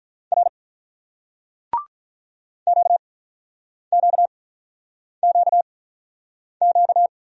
Morse code, heard as a keyed tone, about 46 wpm, spaced as for 9 wpm, 700 Hz. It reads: R QQQQ